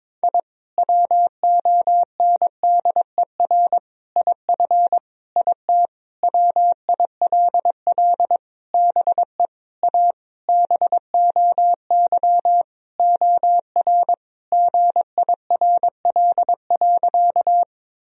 Morse code, heard as a keyed tone, about 22 wpm, 700 Hz. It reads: I WONDER IF IT WILL BE A BOY OR GIRL.
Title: Sentences from Top 1000 Words - Review Part 3 - 22wpm